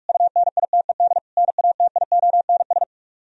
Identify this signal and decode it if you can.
Morse code, heard as a keyed tone, about 49 wpm, 700 Hz. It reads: UNITED NATIONS